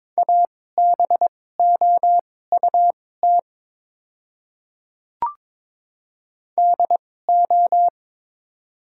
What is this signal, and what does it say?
Morse code, heard as a keyed tone, about 22 wpm, spaced as for 16 wpm, 700 Hz. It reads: ABOUT DO
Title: 2x Character Spacing - Top 100 Words - 22wpm